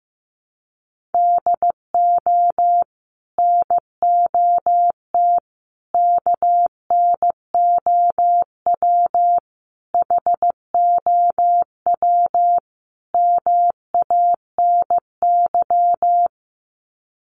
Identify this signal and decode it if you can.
Morse code, heard as a keyed tone, about 15 wpm, 700 Hz. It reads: DO NOT KNOW HOW MANY
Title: Sets of 5 Words Encore 15wpm